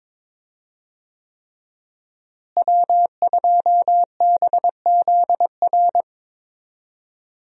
Morse code, heard as a keyed tone, about 22 wpm, 700 Hz. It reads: W2BZR
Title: USA Call Signs - 5 Characters - Repeated 3x - 22wpm